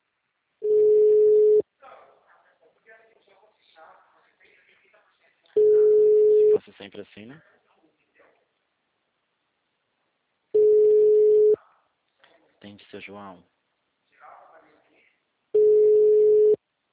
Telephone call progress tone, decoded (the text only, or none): ringback tone